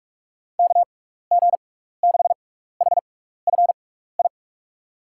Morse code, heard as a keyed tone, about 47 wpm, spaced as for 16 wpm, 700 Hz. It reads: KG6HFI
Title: USA Call Signs - 6 Characters - 6x Character Spacing - 45wpm